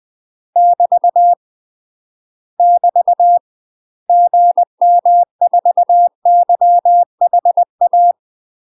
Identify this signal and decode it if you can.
Morse code, heard as a keyed tone, about 20 wpm, 700 Hz. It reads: <BT> <BT> GM4YHA
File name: UK MASTER QSOs TRAINING / 20WPM / UK MASTER QSOs 0058